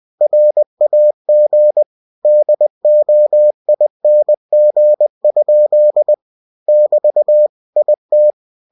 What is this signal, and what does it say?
Morse code, heard as a keyed tone, about 20 wpm, 600 Hz. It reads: RAG DOING? <BT> IT
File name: cw-books / winnie-the-pooh / winnie-20wpm-page-0053